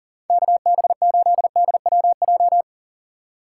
Morse code, heard as a keyed tone, about 49 wpm, 700 Hz. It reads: KB8DWJ
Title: Speed Racing - USA 2x3 Call Signs 40wpm